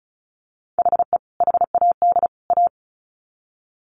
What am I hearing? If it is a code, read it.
Morse code, heard as a keyed tone, about 34 wpm, 700 Hz. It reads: HE HAD A
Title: Sets of 3 Words 35wpm